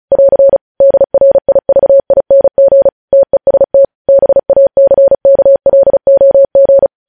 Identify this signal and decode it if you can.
Morse code, heard as a keyed tone, about 36 wpm, 550 Hz. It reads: <AR> DRIVING TEST BACKLOG
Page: News Headlines in Morse Code at 35 WPM